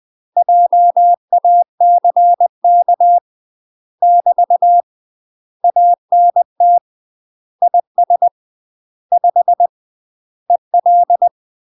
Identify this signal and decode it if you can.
Morse code, heard as a keyed tone, about 20 wpm, 700 Hz. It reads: JACK <BT> ANT IS 5 EL
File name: UK MASTER QSOs TRAINING / 20WPM / UK MASTER QSOs 0040